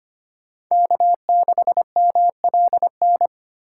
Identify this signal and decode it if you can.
Morse code, heard as a keyed tone, about 25 wpm, 700 Hz. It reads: K6MLN
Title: USA Call Signs - 4x Character Spacing - 25wpm